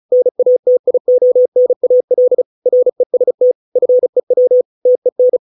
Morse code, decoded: NATIONAL REST FEW TEN

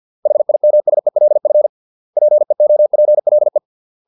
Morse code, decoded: HIMSELF PEOPLE

